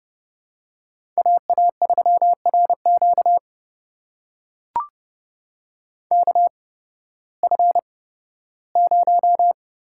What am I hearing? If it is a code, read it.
Morse code, heard as a keyed tone, about 30 wpm, spaced as for 16 wpm, 700 Hz. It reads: AA3RQ KF0
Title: USA Call Signs - 5 Characters - 8x Character Spacing - 30wpm